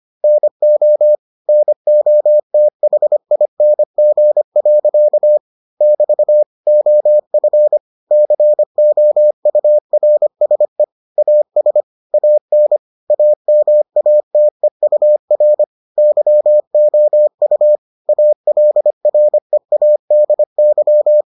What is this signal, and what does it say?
Morse code, heard as a keyed tone, about 25 wpm, 600 Hz. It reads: NO NOTHING. <BT> OF COURSE AS AN AMATEUR YOU ALREADY